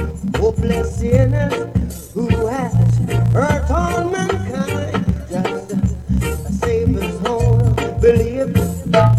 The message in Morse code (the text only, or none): none